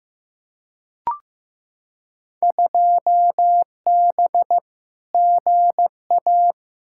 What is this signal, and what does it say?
Morse code, heard as a keyed tone, about 15 wpm, 700 Hz. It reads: 2B GA
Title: Field Day Exchanges Encore - 15wpm